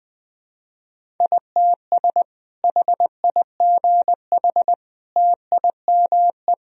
Morse code, heard as a keyed tone, about 21 wpm, 700 Hz. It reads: ITS HIGH TIME
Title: Sentences from Top 1000 Words - Review Part 1 - 20wpm